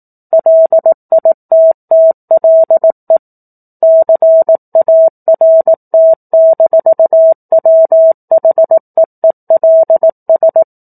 Morse code, heard as a keyed tone, about 18 wpm, 650 Hz. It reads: LITTLE CART-WHEELS